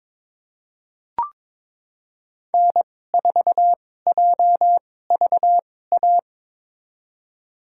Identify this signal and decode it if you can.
Morse code, heard as a keyed tone, about 22 wpm, spaced as for 16 wpm, 700 Hz. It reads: N4JVA
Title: USA Call Signs - 5 Characters - 2x Character Spacing - 22wpm